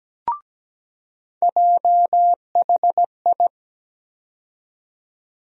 Morse code, heard as a keyed tone, about 17 wpm, 700 Hz. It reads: JHI
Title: USA Call Sign Postfixes - 3 Letters 17wpm